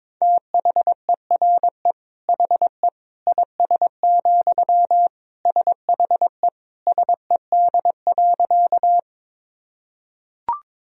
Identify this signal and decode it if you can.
Morse code, heard as a keyed tone, about 22 wpm, 700 Hz. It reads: THERE HE IS, SHE SED.
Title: Sentences from Top 100 Words and Abbreviations 22wpm